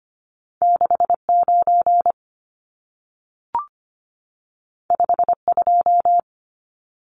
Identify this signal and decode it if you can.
Morse code, heard as a keyed tone, about 25 wpm, 700 Hz. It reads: 69 52
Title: Numbers - 2-Digit 25wpm